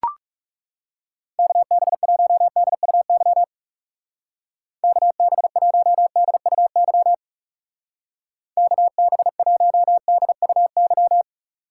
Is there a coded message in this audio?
Yes: Morse code from 1.4 to 11.2 s, heard as a keyed tone, about 39 wpm, 700 Hz. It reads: KB1DUY KB1DUY KB1DUY